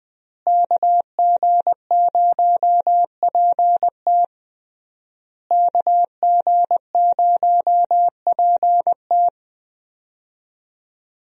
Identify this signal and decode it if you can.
Morse code, heard as a keyed tone, about 20 wpm, 700 Hz. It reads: KG0PT KG0PT
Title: USA Call Signs - 5 Characters - Repeated 3x - 20wpm